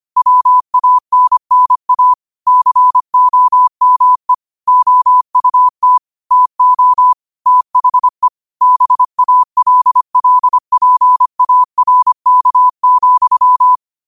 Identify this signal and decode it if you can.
Morse code, heard as a keyed tone, about 25 wpm, 1000 Hz. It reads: WANNA COME OUT TO THE BALLPARK,
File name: qotdfiles / 25wpm